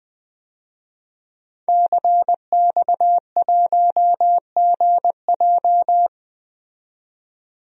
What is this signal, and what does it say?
Morse code, heard as a keyed tone, about 20 wpm, 700 Hz. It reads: CX1GJ
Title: DXCC-100 Call Signs - 5 Characters - 20wpm